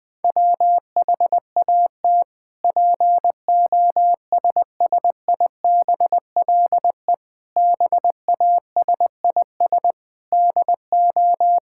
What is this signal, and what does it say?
Morse code, heard as a keyed tone, about 20 wpm, 700 Hz. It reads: WHAT POSSIBLE BASIS DO
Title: Sentences from Top 1000 Words - Review Part 5 - 20wpm